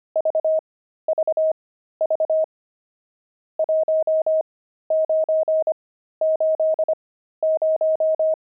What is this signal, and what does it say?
Morse code, heard as a keyed tone, about 25 wpm, spaced as for 13 wpm, 630 Hz. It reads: VVV 1980
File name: pangram-fs-13-cs-25-nums8